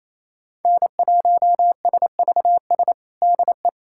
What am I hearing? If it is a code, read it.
Morse code, heard as a keyed tone, about 28 wpm, 700 Hz. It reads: N1SVS DE
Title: USA Callsign DE Callsign - 5 Characters - 28wpm